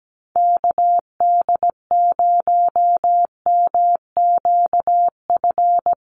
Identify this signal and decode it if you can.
Morse code, heard as a keyed tone, about 17 wpm, 700 Hz. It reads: KD0MQF
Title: USA Call Signs 17wpm